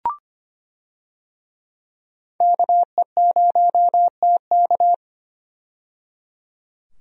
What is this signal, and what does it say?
Morse code, heard as a keyed tone, about 25 wpm, 700 Hz. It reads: KE0TK
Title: USA Call Signs - 5 Characters - 25wpm